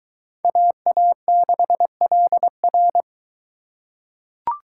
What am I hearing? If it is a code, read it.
Morse code, heard as a keyed tone, about 22 wpm, 700 Hz. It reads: AA6LR